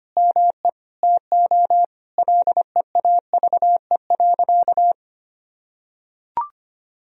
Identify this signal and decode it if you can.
Morse code, heard as a keyed tone, about 25 wpm, 700 Hz. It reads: ME TO LEAVE.
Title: Sentences from Top 1000 Words - Review Part 3 - 25wpm